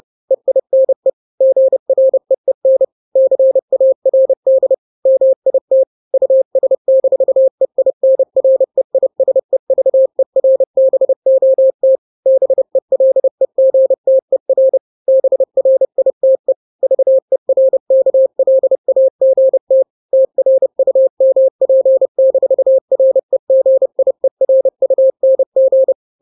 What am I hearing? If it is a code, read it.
Morse code, heard as a keyed tone, about 30 wpm, 530 Hz. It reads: EINE GREEN CARD MIT US-EINREISEVERBOT BELEGTER BRITE VERKLAGT TRUMP-REGIERUNG
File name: daily / 29wpm